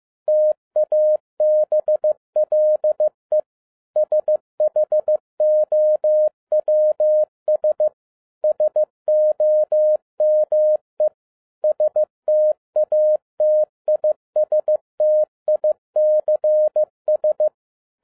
Morse code, heard as a keyed tone, about 15 wpm, 610 Hz. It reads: TABLE SHOWS SOME STATISTICS
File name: 2012-01-mp3 / n17-SUMMARY OF FEA NET IN 2011-15wpm